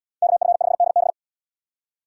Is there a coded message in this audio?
Yes: Morse code from 0.2 to 1.1 s, heard as a keyed tone, about 67 wpm, 700 Hz. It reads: KF5DB